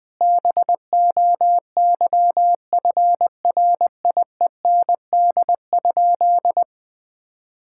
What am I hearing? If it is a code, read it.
Morse code, heard as a keyed tone, about 21 wpm, 700 Hz. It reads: BOYFRIEND?
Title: Sentences from Top 1000 Words - Review Part 5 - 20wpm